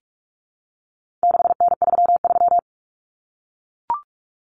Morse code, heard as a keyed tone, about 44 wpm, 700 Hz. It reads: 6N33
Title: Common Cut Numbers - 4-Digit 45wpm